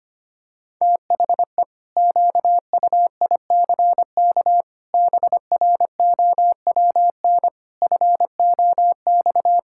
Morse code, heard as a keyed tone, about 25 wpm, 700 Hz. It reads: THE QUICK BROWN FOX
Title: Pangram Challenge v2 - 25wpm